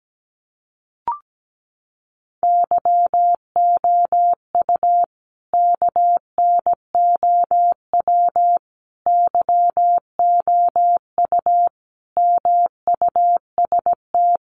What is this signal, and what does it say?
Morse code, heard as a keyed tone, about 17 wpm, 700 Hz. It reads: YOU KNOW YOU MUST
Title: Sentences from Top 500 Words 17wpm